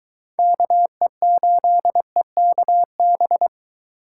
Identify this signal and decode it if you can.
Morse code, heard as a keyed tone, about 23 wpm, 700 Hz. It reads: KE8EKB